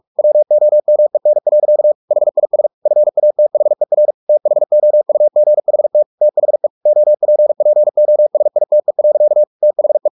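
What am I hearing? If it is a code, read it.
Morse code, heard as a keyed tone, about 44 wpm, 600 Hz. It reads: WOMEN. HIS FATHER THOUGHT THE OPPOSITE. THE
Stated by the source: Morse practice at 45 wpm, the novel Never by Ken Follett in English, from CW med Gnister.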